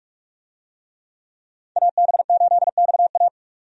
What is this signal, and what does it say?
Morse code, heard as a keyed tone, about 46 wpm, 700 Hz. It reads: AB8XA
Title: USA Call Signs - 5 Characters - 45wpm